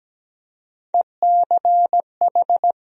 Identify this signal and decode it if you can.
Morse code, heard as a keyed tone, about 17 wpm, 700 Hz. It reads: ECH
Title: USA Call Sign Postfixes - 3 Letters 17wpm